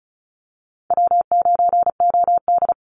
Morse code, heard as a keyed tone, about 34 wpm, 700 Hz. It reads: W9OD
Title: Easy Call Signs Repeated 3x - 35wpm